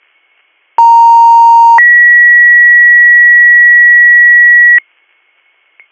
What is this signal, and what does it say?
Two-tone page, heard as a one-second tone, then a three-second tone, about 930 then 1950 Hz. It is heard from 0.8 to 4.8 s.